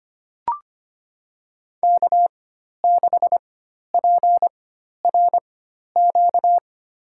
Morse code, heard as a keyed tone, about 25 wpm, spaced as for 12 wpm, 700 Hz. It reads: K6PRQ
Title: USA Call Signs - 5 Characters - 4x Character Spacing - 25wpm